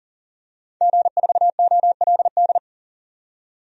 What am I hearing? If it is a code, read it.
Morse code, heard as a keyed tone, about 40 wpm, 700 Hz. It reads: G4OLD
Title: DXCC-100 Call Signs - 5 Characters - 40wpm